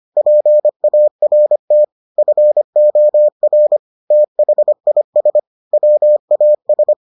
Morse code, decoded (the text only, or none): PART FOR THIS WAS